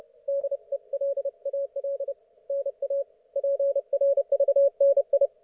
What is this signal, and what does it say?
Morse code, heard as a keyed tone, about 31 wpm, 560 Hz. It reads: DELAL NA PRVNI
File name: Archiv / CUC179 / ok1fhi cul179